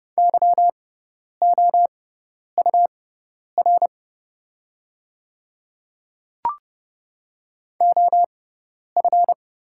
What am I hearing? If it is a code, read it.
Morse code, heard as a keyed tone, about 30 wpm, spaced as for 10 wpm, 690 Hz. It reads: YOUR OF
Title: 6x Character Spacing - Top 100 Words - 30wpm